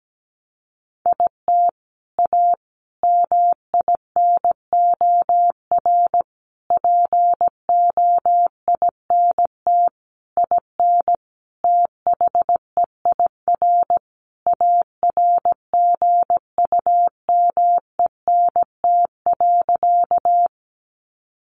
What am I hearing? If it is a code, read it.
Morse code, heard as a keyed tone, about 17 wpm, 700 Hz. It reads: IT A MINOR POINT IN THEIR ARGUMENT.
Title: Sentences from Top 1500 Words 17wpm